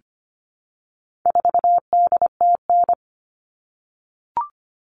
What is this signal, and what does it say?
Morse code, heard as a keyed tone, about 25 wpm, 700 Hz. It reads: 4DTN
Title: All Cut Numbers - 4-Digit 25wpm